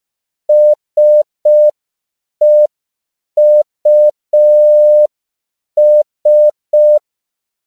Morse code, heard as a keyed tone, about 5 wpm, 600 Hz. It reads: SEUS
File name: CW / 5PPM